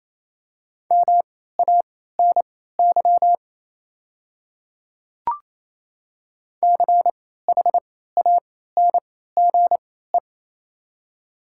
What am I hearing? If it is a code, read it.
Morse code, heard as a keyed tone, about 28 wpm, spaced as for 16 wpm, 700 Hz. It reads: MANY CHANGE
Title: 3x Character Spacing - Top 300 Words - 28wpm